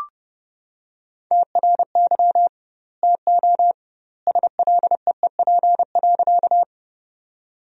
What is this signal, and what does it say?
Morse code, heard as a keyed tone, about 32 wpm, 700 Hz. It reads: TRY TO SLEEP.
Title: 2x Word-Spacing - Sentences from Top 500 Words - 30wpm